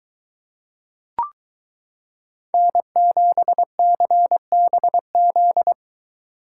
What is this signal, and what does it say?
Morse code, heard as a keyed tone, about 22 wpm, 700 Hz. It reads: N7CBZ